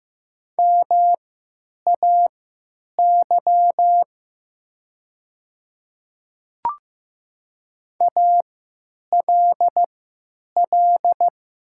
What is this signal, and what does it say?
Morse code, heard as a keyed tone, about 15 wpm, spaced as for 9 wpm, 700 Hz. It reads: MAY ALL
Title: 3x Character Spacing - Top 100 Words - 15wpm